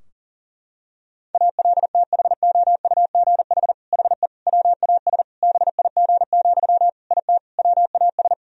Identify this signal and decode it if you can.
Morse code, heard as a keyed tone, about 40 wpm, 700 Hz. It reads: ALTHOUGH HE WAS BIG, IT WAS
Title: Sentences from Top 1000 Words - Review Part 3 - 40wpm